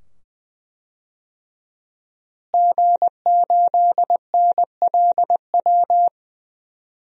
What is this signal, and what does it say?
Morse code, heard as a keyed tone, about 20 wpm, 700 Hz. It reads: G8NLW
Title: DXCC-100 Call Signs - 5 Characters - Repeated 2x - 20wpm